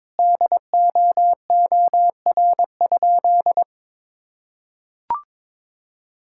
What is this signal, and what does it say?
Morse code, heard as a keyed tone, about 22 wpm, 700 Hz. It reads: DOOR?
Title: Sentences from Top 1600 Words 22wpm